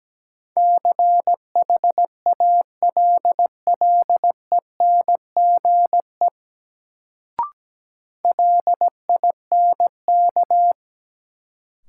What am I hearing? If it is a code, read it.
Morse code, heard as a keyed tone, about 17 wpm, 700 Hz. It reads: CHALLENGE LINK
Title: Sentences from Top 1000 Words - Review Part 7 - New Word Drill - 17wpm